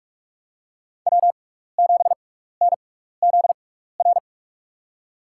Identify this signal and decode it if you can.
Morse code, heard as a keyed tone, about 41 wpm, spaced as for 15 wpm, 700 Hz. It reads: W7NZR
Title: USA Call Signs - 5 Characters - 6x Character Spacing - 45wpm